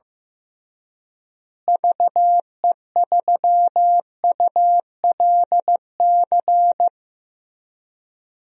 Morse code, heard as a keyed tone, about 15 wpm, 700 Hz. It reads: VE3ULC
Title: DXCC-100 Call Signs - 6 Characters - Repeated 2x - 15wpm